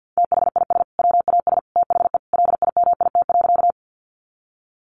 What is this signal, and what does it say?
Morse code, heard as a keyed tone, about 50 wpm, 700 Hz. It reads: THIS WAS THE LIMIT.
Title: Sentences from Top 1500 Words 50wpm